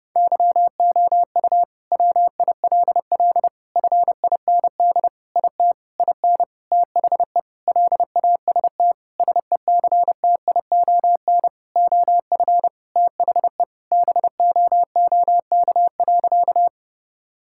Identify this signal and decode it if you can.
Morse code, heard as a keyed tone, about 30 wpm, 700 Hz. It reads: YOU WILL FIND IT IN THE LAST SECTION OF THE BOOK.
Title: Sentences from Top 1000 Words - Review Part 5 - 30wpm